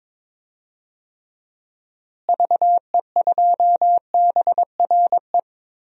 Morse code, heard as a keyed tone, about 22 wpm, 700 Hz. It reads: VE2BRE
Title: DXCC-100 Call Signs - 6 Characters - Repeated 3x - 22wpm